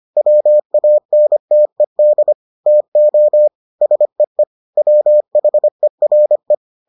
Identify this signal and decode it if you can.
Morse code, heard as a keyed tone, about 25 wpm, 600 Hz. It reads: WANTED TO SEE WHERE